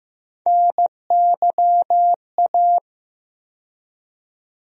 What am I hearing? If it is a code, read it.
Morse code, heard as a keyed tone, about 15 wpm, 700 Hz. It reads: NYA
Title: USA Call Sign Postfixes - 3 Letters 15wpm